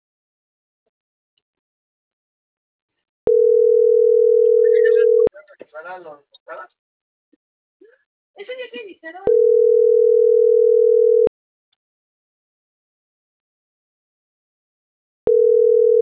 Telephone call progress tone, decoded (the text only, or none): ringback tone